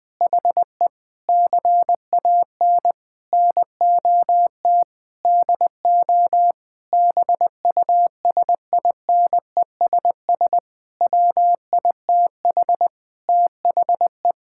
Morse code, heard as a keyed tone, about 19 wpm, 700 Hz. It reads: HE CAN NOT DO BUSINESS WITH THE
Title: Sentences from Top 1000 Words - Review Part 6 - 20wpm